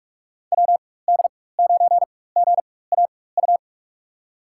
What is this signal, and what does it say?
Morse code, heard as a keyed tone, about 45 wpm, spaced as for 21 wpm, 700 Hz. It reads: WD9GAU
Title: USA Call Signs - 6 Characters - 4x Character Spacing - 45wpm